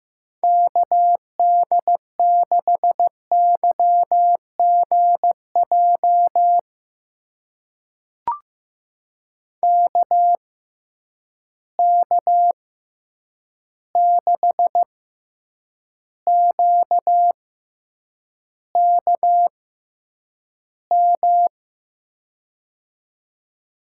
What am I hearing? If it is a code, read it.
Morse code, heard as a keyed tone, about 15 wpm, spaced as for 8 wpm, 700 Hz. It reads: KD6YGJ KK6QKM
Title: USA Call Signs - 6 Characters - 6x Character Spacing - 15wpm